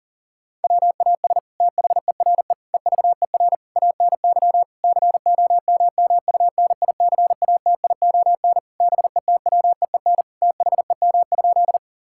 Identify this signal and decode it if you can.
Morse code, heard as a keyed tone, about 40 wpm, 700 Hz. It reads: WAS THERE EVER ANY COMMUNICATION BETWEEN THEM?
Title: Sentences from Top 1000 Words - Review Part 6 - 40wpm